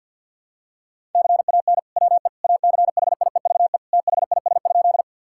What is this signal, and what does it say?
Morse code, heard as a keyed tone, about 47 wpm, 700 Hz. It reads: CAN WE ACHIEVE THIS?
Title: Sentences from Top 1000 Words - Review Part 6 - 50wpm